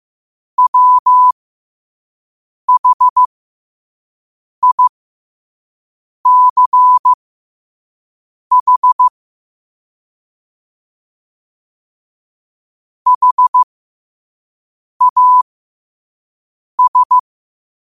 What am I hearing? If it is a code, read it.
Morse code, heard as a keyed tone, about 15 wpm, spaced as for 5 wpm, 1000 Hz. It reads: WHICH HAS
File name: qotdfiles / 5wpm